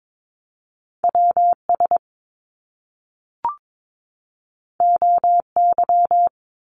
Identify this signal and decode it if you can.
Morse code, heard as a keyed tone, about 22 wpm, 700 Hz. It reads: WS OY
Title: Top Two Letter Combinations in Words 22wpm